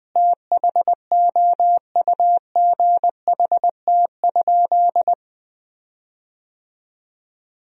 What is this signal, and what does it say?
Morse code, heard as a keyed tone, about 20 wpm, 700 Hz. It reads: THOUGHT?
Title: 4x Word-Spacing - Sentences from Top 300 Words - 20wpm